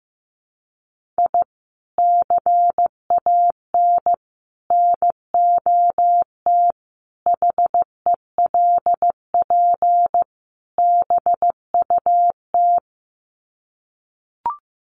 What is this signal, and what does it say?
Morse code, heard as a keyed tone, about 15 wpm, 700 Hz. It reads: I CAN NOT HELP BUT
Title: Sets of 5 Words Encore2 15wpm